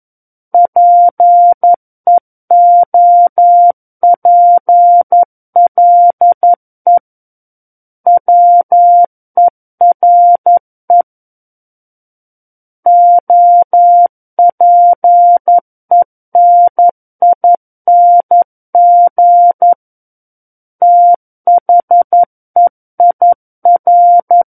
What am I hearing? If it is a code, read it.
Morse code, heard as a keyed tone, about 11 wpm, 700 Hz. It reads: PEOPLE WERE OPENING THEIR
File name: Book One / 02-Chapter 2 - 11 WPM